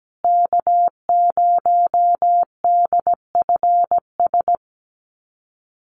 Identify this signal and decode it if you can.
Morse code, heard as a keyed tone, about 17 wpm, 700 Hz. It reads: K0DFS
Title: USA Call Signs Repeated 3x 17wpm